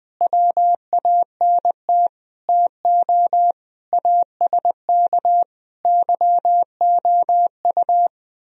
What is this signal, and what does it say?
Morse code, heard as a keyed tone, about 20 wpm, 700 Hz. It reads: WANT TO ASK YOU
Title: Speed-Racing - Sets of 5 Words 20wpm